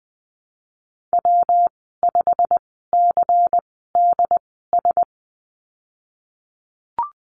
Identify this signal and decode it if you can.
Morse code, heard as a keyed tone, about 20 wpm, spaced as for 14 wpm, 700 Hz. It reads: W5CDS
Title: 173 Lesson 28g - Number 9 - Call Signs 20wpm